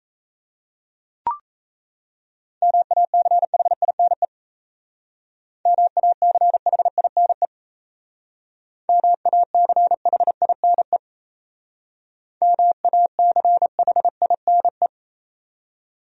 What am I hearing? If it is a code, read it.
Morse code, heard as a keyed tone, about 35 wpm, 700 Hz. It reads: MACHINE MACHINE MACHINE MACHINE